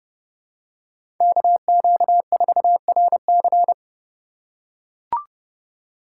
Morse code, heard as a keyed tone, about 30 wpm, 700 Hz. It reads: KQ4RC
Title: USA Call Signs - 5 Characters - 4x Character Spacing - 30wpm